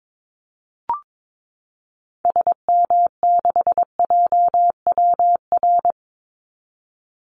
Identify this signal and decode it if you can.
Morse code, heard as a keyed tone, about 22 wpm, 700 Hz. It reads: SM6JWR
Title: Call Signs Repeated 3x - 22wpm